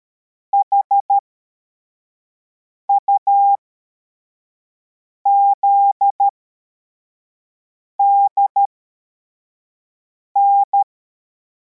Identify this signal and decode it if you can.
Morse code, heard as a keyed tone, about 13 wpm, spaced as for 4 wpm, 800 Hz. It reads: HUZDN